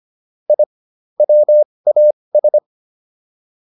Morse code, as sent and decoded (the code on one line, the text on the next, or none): .. / .-- .- ...
I WAS